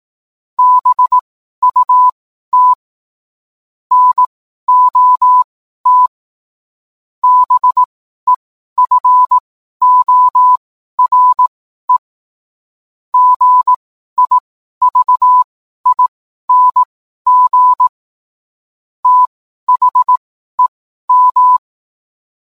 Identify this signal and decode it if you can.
Morse code, heard as a keyed tone, about 18 wpm, spaced as for 12 wpm, 1000 Hz. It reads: BUT NOT BEFORE GIVING THEM